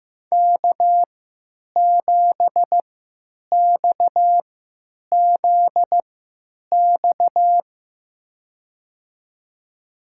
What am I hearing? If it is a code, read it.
Morse code, heard as a keyed tone, about 15 wpm, spaced as for 9 wpm, 700 Hz. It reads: K7XZX
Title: USA Call Signs - 5 Characters - 3x Character Spacing - 15wpm